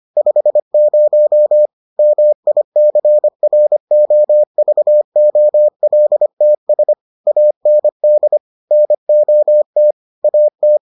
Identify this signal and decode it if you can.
Morse code, heard as a keyed tone, about 25 wpm, 600 Hz. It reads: 50 MICROVOLTS AND NOT AT